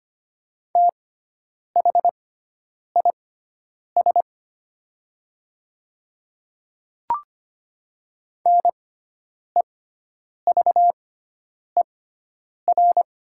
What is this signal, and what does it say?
Morse code, heard as a keyed tone, about 25 wpm, spaced as for 9 wpm, 700 Hz. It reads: THIS NEVER
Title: 6x Character Spacing - Top 300 Words - 25wpm